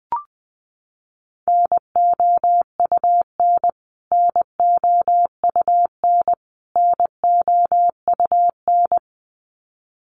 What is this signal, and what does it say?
Morse code, heard as a keyed tone, about 20 wpm, 700 Hz. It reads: NOUN NOUN NOUN
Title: Top 500 Words Repeated 20wpm